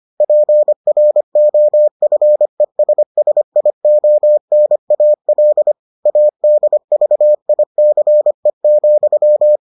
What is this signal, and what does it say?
Morse code, heard as a keyed tone, about 25 wpm, 600 Hz. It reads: PROFESSIONAL ADVICE,